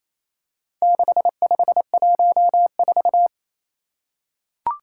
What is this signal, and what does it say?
Morse code, heard as a keyed tone, about 28 wpm, 700 Hz. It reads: 6514
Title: Four Challenging Letter-Number Combinations 28wpm